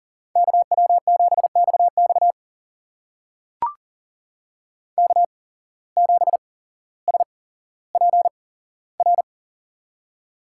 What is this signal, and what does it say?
Morse code, heard as a keyed tone, about 39 wpm, spaced as for 17 wpm, 700 Hz. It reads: KW7XX K7SPR
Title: USA Call Signs - 5 Characters - 8x Character Spacing - 40wpm